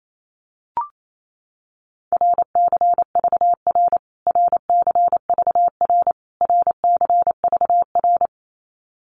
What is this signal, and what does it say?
Morse code, heard as a keyed tone, about 28 wpm, 700 Hz. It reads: RCVR RCVR RCVR
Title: Abbreviations Repeated 3x - 28wpm